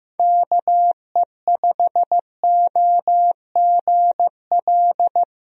Morse code, heard as a keyed tone, about 16 wpm, 700 Hz. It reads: KE5OGL
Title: USA Call Signs - 6 Characters - Repeated 2x - 15wpm